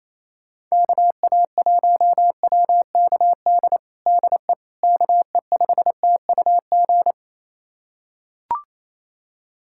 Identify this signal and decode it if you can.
Morse code, heard as a keyed tone, about 28 wpm, 700 Hz. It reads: KA1WKD DE KE5TUG
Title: USA Callsign DE Callsign - 6 Characters - 28wpm